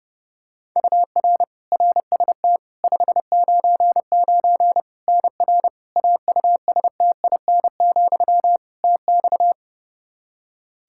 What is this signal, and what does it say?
Morse code, heard as a keyed tone, about 32 wpm, 700 Hz. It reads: UR RST 599 NR AUSTIN, TX
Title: Practice QSO - Exchange 1 - Beginning - 30wpm